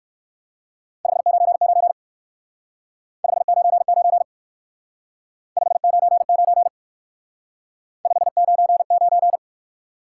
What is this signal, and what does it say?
Morse code, heard as a keyed tone, about 55 wpm, 700 Hz. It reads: 599 599 599 599
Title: Speed-Racing Top 90 QSO Elements 45wpm